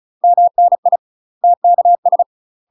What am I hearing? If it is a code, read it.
Morse code, heard as a keyed tone, about 34 wpm, 700 Hz. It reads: MNI TKS
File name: UK MASTER QSOs TRAINING / 35WPM / UK MASTER QSOs 0043